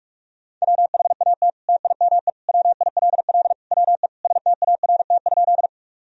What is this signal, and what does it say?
Morse code, heard as a keyed tone, about 42 wpm, 700 Hz. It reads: WHAT TIME WILL WE START?
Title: Sentences from Top 1000 Words - Review Part 1 - 45wpm